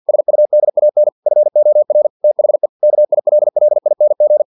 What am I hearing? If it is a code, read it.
Morse code, heard as a keyed tone, about 50 wpm, 600 Hz. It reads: SUDAN FOR THE KILLING
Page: Never 49wpm